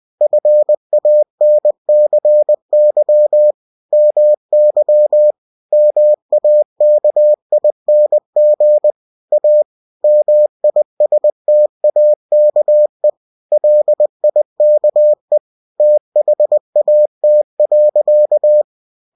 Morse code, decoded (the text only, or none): FANCY MY MAKING A MISTAKE LIKE THAT.